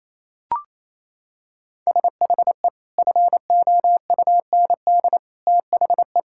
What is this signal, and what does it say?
Morse code, heard as a keyed tone, about 28 wpm, 700 Hz. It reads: SHE FOUND THE
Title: Sentences from Top 1000 Words - Review Part 6 - 28wpm